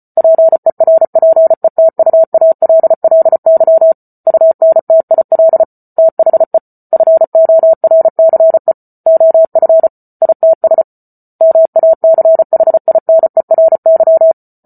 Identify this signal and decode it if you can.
Morse code, heard as a keyed tone, about 34 wpm, 650 Hz. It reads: PERPETUALLY UNTIL THE FORCE OF ITS MACHINERY